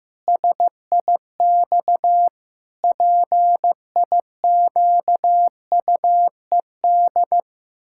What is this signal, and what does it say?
Morse code, heard as a keyed tone, about 16 wpm, 700 Hz. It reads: SIX PIQUED
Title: Pangram Challenge v2 - 15wpm